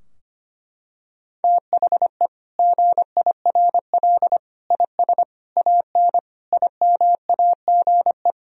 Morse code, heard as a keyed tone, about 25 wpm, 700 Hz. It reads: THE GIRL IS AN IMAGE
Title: Sentences from Top 1000 Words - Review Part 4 - 25wpm